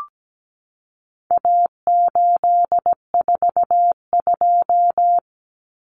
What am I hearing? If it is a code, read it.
Morse code, heard as a keyed tone, about 17 wpm, 700 Hz. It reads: A842
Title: Common Cut Numbers - 4-Digit 17wpm